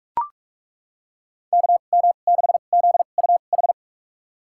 Morse code, heard as a keyed tone, about 41 wpm, spaced as for 31 wpm, 700 Hz. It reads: KM6ZUH
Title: USA Call Signs - 6 Characters - 2x Character Spacing - 45wpm